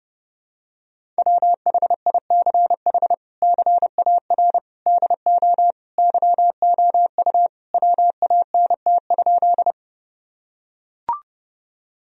Morse code, heard as a keyed tone, about 30 wpm, 700 Hz. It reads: WHICH CAR DO YOU WANT?
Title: Sentences from Top 1000 Words - Review Part 1 - 30wpm